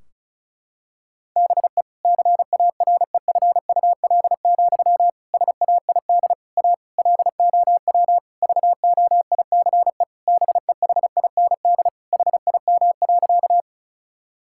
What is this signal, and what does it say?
Morse code, heard as a keyed tone, about 36 wpm, 700 Hz. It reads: BE CAREFUL, SAID A LOW VOICE BEHIND HIM.
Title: Sentences from Top 1600 Words 35wpm